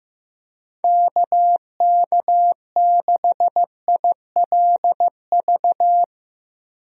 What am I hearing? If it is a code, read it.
Morse code, heard as a keyed tone, about 15 wpm, 700 Hz. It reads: KK6ILV
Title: USA Call Signs - 6 Characters - Repeated 2x - 15wpm